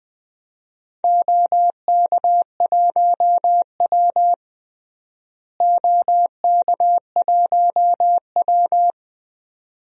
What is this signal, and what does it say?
Morse code, heard as a keyed tone, about 20 wpm, 700 Hz. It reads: OK1W OK1W
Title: DXCC-100 Call Signs - 4 Characters - Repeated 2x - 20wpm